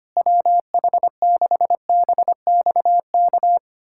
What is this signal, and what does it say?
Morse code, heard as a keyed tone, about 25 wpm, 700 Hz. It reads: WH6BXK
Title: USA Callsign DE Callsign - 6 Characters - 25wpm